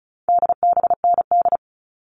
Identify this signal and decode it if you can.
Morse code, heard as a keyed tone, about 34 wpm, 700 Hz. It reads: DBND
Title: All Cut Numbers - 4-Digit 35wpm